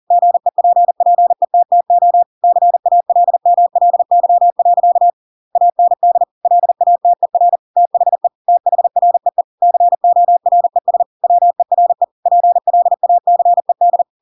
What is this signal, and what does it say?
Morse code, heard as a keyed tone, about 40 wpm, 700 Hz. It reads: GEPPETTO CALMLY. AND LATER THE THREE CORES WERE PLACED
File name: Pinocchio / Pinocchio ch7 40 40 ik1hge